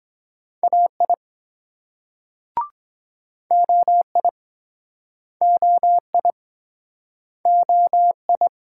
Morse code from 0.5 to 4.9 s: AI OI

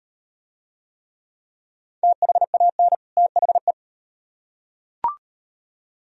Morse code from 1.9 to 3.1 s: THAN

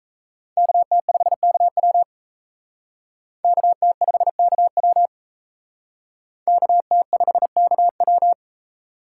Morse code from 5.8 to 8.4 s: KT5KW

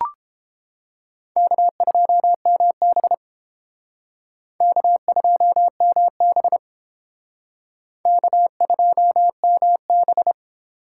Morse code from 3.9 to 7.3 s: K2MB

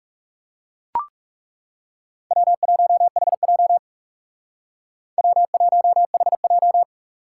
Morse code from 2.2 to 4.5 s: W1HJ

